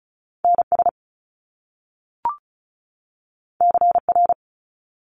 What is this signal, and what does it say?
Morse code, heard as a keyed tone, about 36 wpm, 700 Hz. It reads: NS CR
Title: Top Two Letter Combinations in Words 35wpm